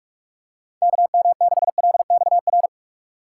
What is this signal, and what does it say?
Morse code, heard as a keyed tone, about 44 wpm, 700 Hz. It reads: KM6LXR